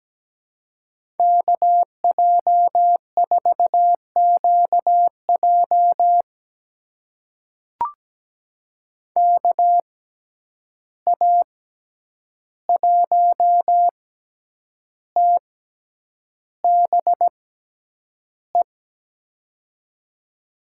Morse code, heard as a keyed tone, about 17 wpm, spaced as for 8 wpm, 700 Hz. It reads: KJ4QJ KA1TBE
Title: USA Call Signs - 6x Character Spacing - 17wpm